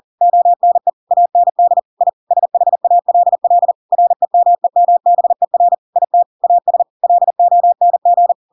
Morse code, heard as a keyed tone, about 40 wpm, 700 Hz. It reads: ONE AND I SHALL REMEMBER IT AS LONG
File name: Pinocchio / Pinocchio ch7 40 40 ik1hge